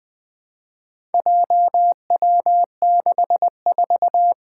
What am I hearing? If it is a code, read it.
Morse code, heard as a keyed tone, about 20 wpm, 700 Hz. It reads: JW64
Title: Four Challenging Letter-Number Combinations 20wpm